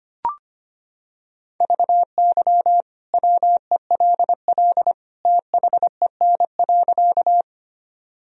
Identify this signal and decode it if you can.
Morse code, heard as a keyed tone, about 25 wpm, 700 Hz. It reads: VY WELL THEN.
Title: Sentences from Top 200 Words and Abbreviations 25wpm